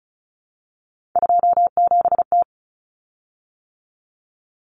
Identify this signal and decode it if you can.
Morse code, heard as a keyed tone, about 34 wpm, 700 Hz. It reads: 27T
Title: Common Cut Numbers - 3-Digit 35wpm